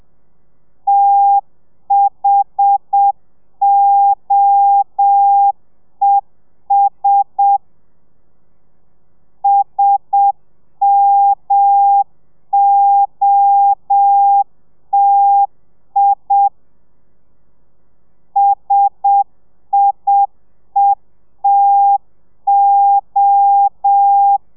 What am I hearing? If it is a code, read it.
Morse code, heard as a keyed tone, about 7 wpm, 800 Hz. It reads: THOES SMOTI SIETO
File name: telegrafia / i2tfj / Lez1 30